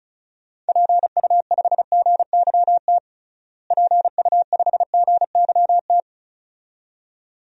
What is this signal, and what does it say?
Morse code, heard as a keyed tone, about 34 wpm, 700 Hz. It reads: PU5GYT PU5GYT
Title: DXCC-100 Call Signs - 6 Characters - Repeated 2x - 35wpm